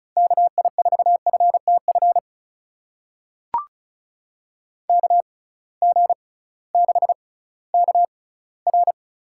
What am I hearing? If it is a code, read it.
Morse code, heard as a keyed tone, about 36 wpm, spaced as for 19 wpm, 700 Hz. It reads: KI4FTF KG6KR
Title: USA Call Signs - 6 Characters - 6x Character Spacing - 35wpm